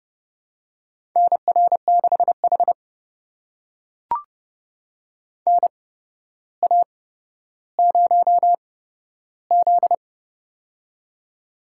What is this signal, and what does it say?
Morse code, heard as a keyed tone, about 32 wpm, spaced as for 13 wpm, 700 Hz. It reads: NR6H NA0Z
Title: USA Call Signs - 4 Characters - 8x Character Spacing - 30wpm